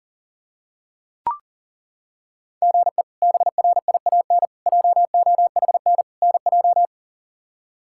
Morse code, heard as a keyed tone, about 40 wpm, 700 Hz. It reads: GE BRIAN JOHN NJ
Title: K1USN SST QSO Practice - 40wpm